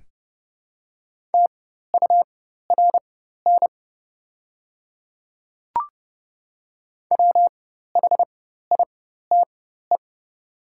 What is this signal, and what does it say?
Morse code, heard as a keyed tone, about 30 wpm, spaced as for 14 wpm, 700 Hz. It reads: TURN WHITE
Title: 4x Character Spacing - Top 500 Words - 30wpm